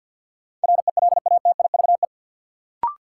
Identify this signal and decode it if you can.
Morse code, heard as a keyed tone, about 52 wpm, 700 Hz. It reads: RELATIVE
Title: Sentences from Top 1400 Words - New Word Drill Alt - 50wpm